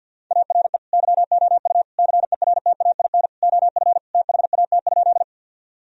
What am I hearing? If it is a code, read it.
Morse code, heard as a keyed tone, about 47 wpm, 700 Hz. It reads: ARE YOU CERTAIN OF THAT?
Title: Sentences from Top 1000 Words - Review Part 2 - 50wpm